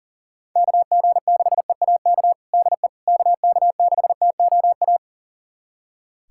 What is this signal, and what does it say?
Morse code, heard as a keyed tone, about 40 wpm, 700 Hz. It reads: KG6EAK DE KK6TOA
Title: USA Callsign DE Callsign - 6 Characters - 40wpm